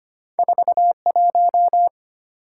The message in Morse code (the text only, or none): 41